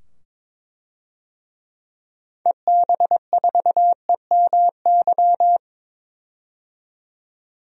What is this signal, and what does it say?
Morse code, heard as a keyed tone, about 22 wpm, 700 Hz. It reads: EB4EMY
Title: DXCC-100 Call Signs - 6 Characters - 22wpm